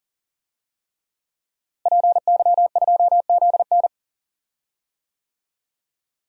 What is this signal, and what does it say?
Morse code, heard as a keyed tone, about 40 wpm, 700 Hz. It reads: PY2ZN
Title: DXCC-100 Call Signs - 5 Characters - Repeated 3x - 40wpm